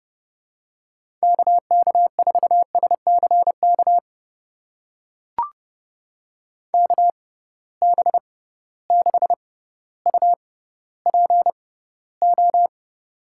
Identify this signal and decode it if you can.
Morse code, heard as a keyed tone, about 31 wpm, spaced as for 15 wpm, 700 Hz. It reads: KK4SCK KB6UPO
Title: USA Call Signs - 6 Characters - 6x Character Spacing - 30wpm